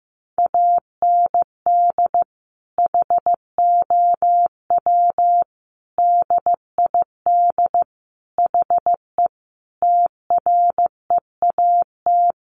Morse code, heard as a keyed tone, about 15 wpm, 700 Hz. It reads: AND HOW DID HE TREAT
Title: Sentences from Top 1300 Words 15wpm